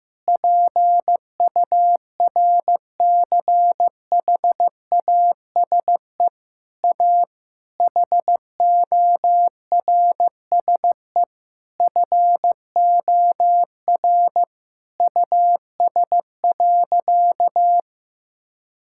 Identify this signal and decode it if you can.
Morse code, heard as a keyed tone, about 15 wpm, 700 Hz. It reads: PURCHASE A HORSE FOR US.
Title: Sentences from Top 1000 Words - Review Part 5 - 15wpm